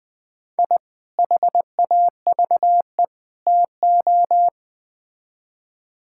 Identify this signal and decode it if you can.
Morse code, heard as a keyed tone, about 20 wpm, 700 Hz. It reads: I HAVE TO
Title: Build A Phrase - 3-Word Phrases - 20wpm